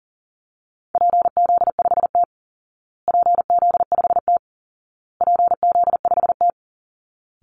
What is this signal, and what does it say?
Morse code, heard as a keyed tone, about 40 wpm, 700 Hz. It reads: PZ5T PZ5T PZ5T
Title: Call Signs Repeated 2x - 40wpm